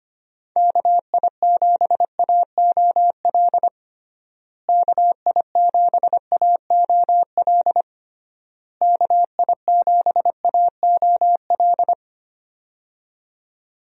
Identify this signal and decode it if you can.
Morse code, heard as a keyed tone, about 25 wpm, 700 Hz. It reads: KI7AOL KI7AOL KI7AOL
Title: USA Call Signs - 6 Characters - Repeated 3x - 25wpm